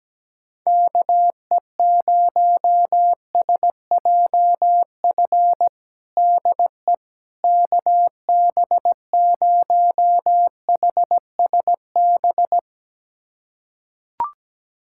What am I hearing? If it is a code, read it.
Morse code, heard as a keyed tone, about 17 wpm, 700 Hz. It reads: KE0SJF DE KB0HSB
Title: USA Callsign DE Callsign - 6 Characters - 17wpm